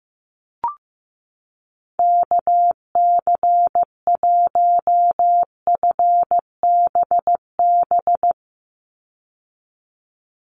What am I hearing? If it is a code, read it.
Morse code, heard as a keyed tone, about 15 wpm, 700 Hz. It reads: KC1FBB
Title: USA Call Signs 15wpm